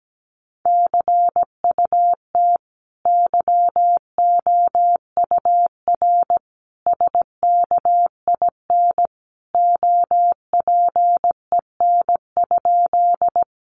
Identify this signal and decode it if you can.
Morse code, heard as a keyed tone, about 17 wpm, 700 Hz. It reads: CUT YOUR SKIN OPEN?
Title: Sentences from Top 1300 Words 17wpm